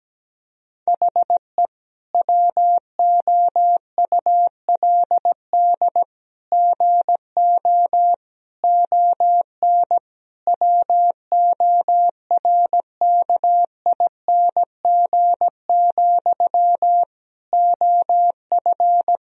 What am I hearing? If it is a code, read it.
Morse code, heard as a keyed tone, about 17 wpm, 700 Hz. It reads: HE WOULD GO ON WORKING, OF
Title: Sentences from Top 1000 Words - Review Part 4 - 17wpm